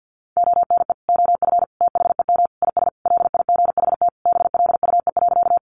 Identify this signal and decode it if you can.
Morse code, heard as a keyed tone, about 50 wpm, 700 Hz. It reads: ONE OF THEM IS LIGHT BLUE.
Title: Additional Easy Phrases 50wpm